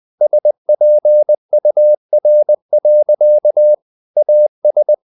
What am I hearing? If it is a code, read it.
Morse code, heard as a keyed tone, about 20 wpm, 600 Hz. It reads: SPUR. AS
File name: the old man and the sea / the old man and the sea part 1 20wpm